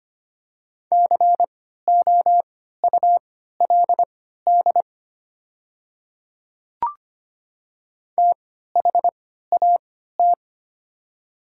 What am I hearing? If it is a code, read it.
Morse code, heard as a keyed tone, about 25 wpm, spaced as for 14 wpm, 700 Hz. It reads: COULD THAT
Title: 3x Character Spacing - Top 100 Words - 25wpm